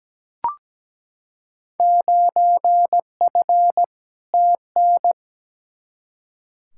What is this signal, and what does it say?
Morse code, heard as a keyed tone, about 17 wpm, 700 Hz. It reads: 9F TN
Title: Field Day Exchanges Encore - 17wpm